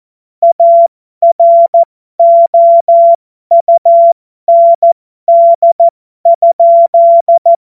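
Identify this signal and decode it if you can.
Morse code, heard as a keyed tone, about 14 wpm, spaced as for 12 wpm, 680 Hz. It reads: AROUND?
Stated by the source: Morse joke practice at effective 60, Farnsworth 70.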